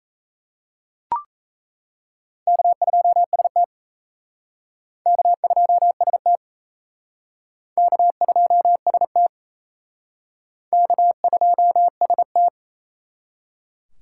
Morse code, heard as a keyed tone, about 34 wpm, 700 Hz. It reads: K2ST K2ST K2ST K2ST
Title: Speed Racing - USA Call Signs 28wpm